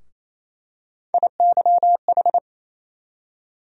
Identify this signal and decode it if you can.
Morse code, heard as a keyed tone, about 28 wpm, 700 Hz. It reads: IYH
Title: USA Call Sign Postfixes - 3 Letters 28wpm